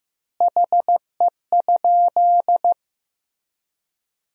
Morse code, heard as a keyed tone, about 15 wpm, 700 Hz. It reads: HE?
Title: 3x Word-Spacing - Sentences from Top 200 Words - 15wpm